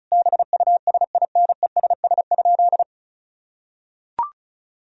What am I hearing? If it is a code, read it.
Morse code, heard as a keyed tone, about 34 wpm, 700 Hz. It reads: BUSINESS?
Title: Sentences from Top 1000 Words - Review Part 4 - 35wpm